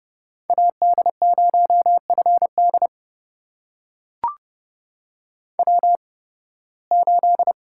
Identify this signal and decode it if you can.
Morse code, heard as a keyed tone, about 30 wpm, spaced as for 20 wpm, 700 Hz. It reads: AD0FD W8
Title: USA Call Signs - 5 Characters - 8x Character Spacing - 30wpm